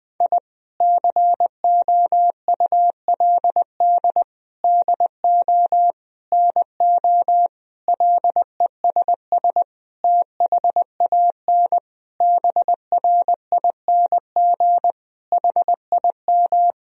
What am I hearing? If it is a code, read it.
Morse code, heard as a keyed tone, about 20 wpm, 700 Hz. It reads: I COULD DO NO LESS THAN BRING HIM
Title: Sentences from Top 1000 Words - Review Part 3 - 20wpm